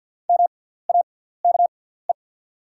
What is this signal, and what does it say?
Morse code, heard as a keyed tone, about 48 wpm, spaced as for 17 wpm, 690 Hz. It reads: MAKE
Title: 6x Character Spacing - Top 100 Words - 50wpm